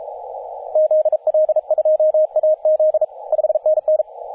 Morse code, heard as a keyed tone, about 33 wpm, 620 Hz, with radio noise. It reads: ZL2AZ 5NN
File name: t30cow-18-072mhz-cw